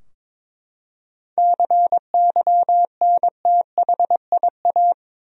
Morse code, heard as a keyed tone, about 22 wpm, 700 Hz. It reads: CYNTHIA
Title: USA Common Names - 22wpm